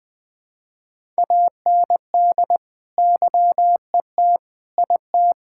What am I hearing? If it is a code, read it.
Morse code, heard as a keyed tone, about 20 wpm, 700 Hz. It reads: AND YET IT
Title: Sentences from Top 1000 Words - Review Part 5 - 20wpm